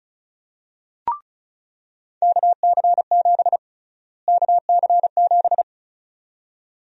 Morse code, heard as a keyed tone, about 34 wpm, 700 Hz. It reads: KC7 KC7
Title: USA Call Sign Prefixes - Repeated 2x - 35wpm